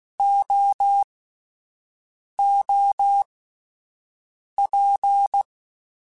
Morse code, heard as a keyed tone, about 16 wpm, spaced as for 5 wpm, 800 Hz. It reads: OOP
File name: Misc / Misc-mp3s / Jumps @5